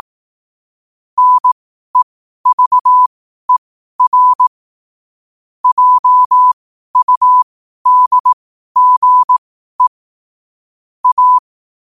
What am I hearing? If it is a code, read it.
Morse code, heard as a keyed tone, about 18 wpm, spaced as for 12 wpm, 1000 Hz. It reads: NEVER JUDGE A
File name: qotdfiles / 12wpm